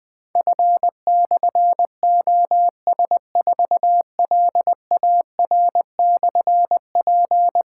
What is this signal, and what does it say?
Morse code, heard as a keyed tone, about 20 wpm, 700 Hz. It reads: F/OS4LAR/P
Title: Difficult Call Signs 20wpm